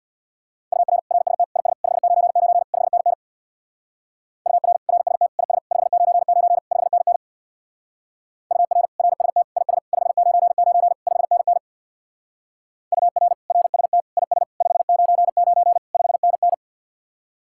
Morse code, heard as a keyed tone, about 64 wpm, 700 Hz. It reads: UR RST IS 599 5NN UR RST IS 599 5NN UR RST IS 599 5NN UR RST IS 599 5NN